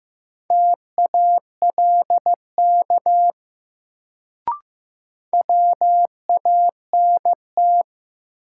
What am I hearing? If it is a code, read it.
Morse code, heard as a keyed tone, about 15 wpm, 700 Hz. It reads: TALK WANT
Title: Sets of 2 Words - New Word Drill - 15wpm